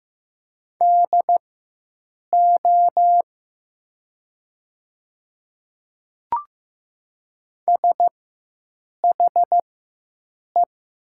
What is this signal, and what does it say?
Morse code, heard as a keyed tone, about 15 wpm, spaced as for 7 wpm, 700 Hz. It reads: DO SHE
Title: 4x Character Spacing - Top 100 Words - 15wpm